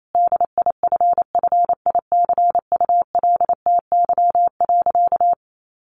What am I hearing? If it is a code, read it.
Morse code, heard as a keyed tone, about 28 wpm, 700 Hz. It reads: DIFFICULTY.
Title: Sentences from Top 1500 Words 28wpm